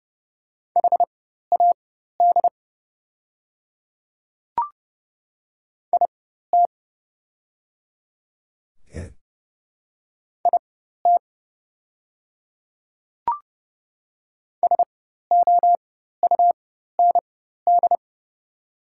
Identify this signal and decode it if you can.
Morse code, heard as a keyed tone, about 30 wpm, spaced as for 14 wpm, 700 Hz. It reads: HAD IT IT SOUND
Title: 4x Character Spacing - Top 100 Words - 30wpm